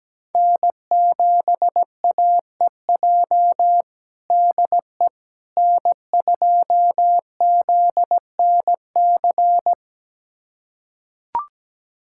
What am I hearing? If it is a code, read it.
Morse code, heard as a keyed tone, about 17 wpm, 700 Hz. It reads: N7AEJ DE N2ZNC
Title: USA Callsign DE Callsign 17wpm